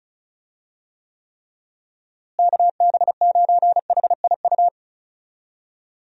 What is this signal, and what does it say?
Morse code, heard as a keyed tone, about 36 wpm, 700 Hz. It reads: KB9HIU
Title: USA Call Signs - 6 Characters - Repeated 3x - 35wpm